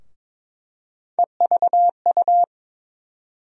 Morse code, heard as a keyed tone, about 22 wpm, 700 Hz. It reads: EVU
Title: USA Call Sign Postfixes - 3 Letters 22wpm